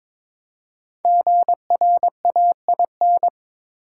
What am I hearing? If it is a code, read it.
Morse code, heard as a keyed tone, about 22 wpm, 690 Hz. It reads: GRAIN
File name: 5 Letter Words - First 3 Letters Given - 22wpm